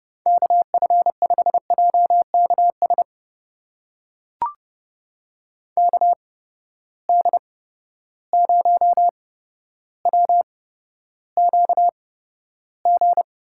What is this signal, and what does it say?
Morse code, heard as a keyed tone, about 30 wpm, spaced as for 13 wpm, 700 Hz. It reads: KF5JKS KD0WQG
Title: USA Call Signs - 6 Characters - 8x Character Spacing - 30wpm